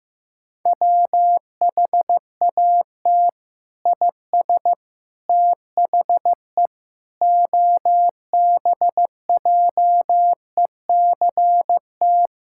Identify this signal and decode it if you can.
Morse code, heard as a keyed tone, about 15 wpm, 700 Hz. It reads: WHAT IS THE OBJECT
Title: Sentences from Top 1000 Words - Review Part 6 - 15wpm